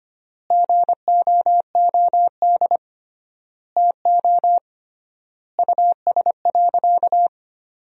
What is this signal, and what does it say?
Morse code, heard as a keyed tone, about 25 wpm, 700 Hz. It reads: GOOD TO US.
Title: 3x Word-Spacing - Sentences from Top 200 Words - 25wpm